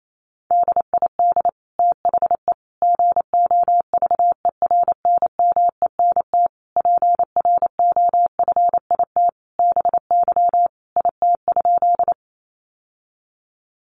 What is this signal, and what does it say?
Morse code, heard as a keyed tone, about 28 wpm, 700 Hz. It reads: DID THE GOVERNMENT PROFIT BY IT?
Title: Sentences from Top 800 Words 28wpm